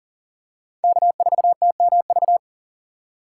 Morse code, heard as a keyed tone, about 40 wpm, 700 Hz. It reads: K4TMV
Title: USA Call Signs - 6x Character Spacing - 40wpm